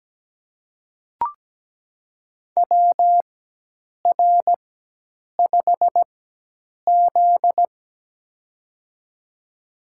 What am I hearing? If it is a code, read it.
Morse code, heard as a keyed tone, about 17 wpm, spaced as for 8 wpm, 700 Hz. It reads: WR5Z
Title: USA Call Signs - 4 Characters - 4x Character Spacing - 17wpm